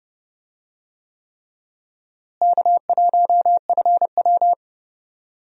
Morse code, heard as a keyed tone, about 31 wpm, 700 Hz. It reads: K1FW
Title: USA Call Signs - 4 Characters - Repeated 3x - 30wpm